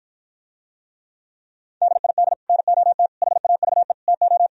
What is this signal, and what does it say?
Morse code, heard as a keyed tone, about 54 wpm, 700 Hz. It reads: DID NOT HAVE TO